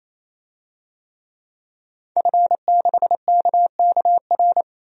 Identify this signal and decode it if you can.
Morse code, heard as a keyed tone, about 27 wpm, 700 Hz. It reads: F6KKR
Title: DXCC-100 Call Signs - 5 Characters - 28wpm